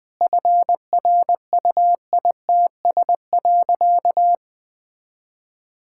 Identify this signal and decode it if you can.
Morse code, heard as a keyed tone, about 20 wpm, 700 Hz. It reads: FRUITS.
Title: Sentences with 7-Word Phrases - 20wpm